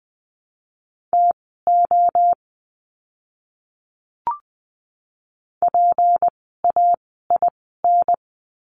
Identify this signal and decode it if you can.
Morse code, heard as a keyed tone, about 20 wpm, spaced as for 15 wpm, 700 Hz. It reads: TO PAIN
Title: 153 Lesson 25f - Letter p - Words 20wpm